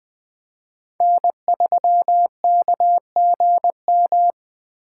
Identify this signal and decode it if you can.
Morse code, heard as a keyed tone, about 20 wpm, 700 Hz. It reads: N3KGM
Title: Speed Racing - USA 1x3 Call Signs 17wpm